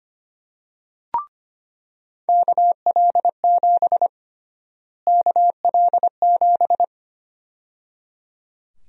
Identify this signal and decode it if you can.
Morse code, heard as a keyed tone, about 25 wpm, 700 Hz. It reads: KL7 KL7
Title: USA Call Sign Prefixes - Repeated 2x - 25wpm